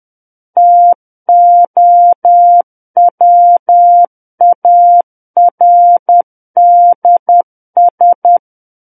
Morse code, heard as a keyed tone, about 10 wpm, 700 Hz. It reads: TOWARDS
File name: Book One / 01-Chapter 1 - 10 WPM